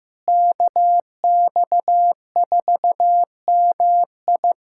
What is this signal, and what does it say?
Morse code, heard as a keyed tone, about 15 wpm, 700 Hz. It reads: KX4MI
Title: K1USN SST QSO Practice - 15wpm